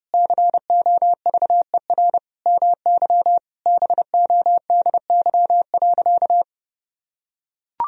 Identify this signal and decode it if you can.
Morse code, heard as a keyed tone, about 30 wpm, 700 Hz. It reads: COVER MY BODY.
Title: Sentences from Top 1000 Words - Review Part 1 - 30wpm